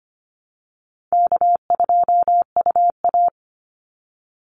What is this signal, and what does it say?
Morse code, heard as a keyed tone, about 25 wpm, 700 Hz. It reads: K2UA
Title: Easy Call Signs Repeated 2x - 25wpm